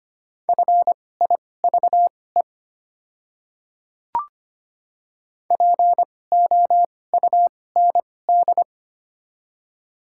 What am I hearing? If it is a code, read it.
Morse code, heard as a keyed tone, about 25 wpm, spaced as for 18 wpm, 700 Hz. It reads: FIVE POUND
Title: 2x Character Spacing - Top 500 Words - 25wpm